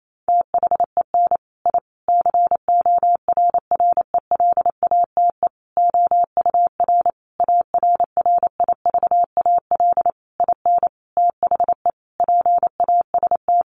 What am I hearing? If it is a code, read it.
Morse code, heard as a keyed tone, about 28 wpm, 700 Hz. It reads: THEN I CORRELATE OUR ARRIVAL IN THE PAST
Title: A Sound Of Thunder 28wpm